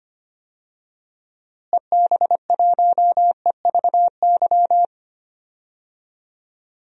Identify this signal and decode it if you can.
Morse code, heard as a keyed tone, about 25 wpm, 700 Hz. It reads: EB1EVY
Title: DXCC-100 Call Signs - 6 Characters - Repeated 3x - 25wpm